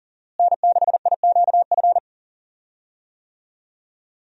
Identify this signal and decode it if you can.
Morse code, heard as a keyed tone, about 40 wpm, 700 Hz. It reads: N6IQF
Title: USA Call Signs - 5 Characters - Repeated 3x - 40wpm